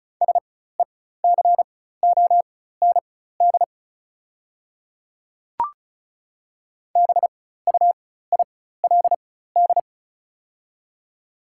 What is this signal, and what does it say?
Morse code, heard as a keyed tone, about 36 wpm, spaced as for 16 wpm, 690 Hz. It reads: SECOND BUILD
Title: 4x Character Spacing - Top 300 Words - 35wpm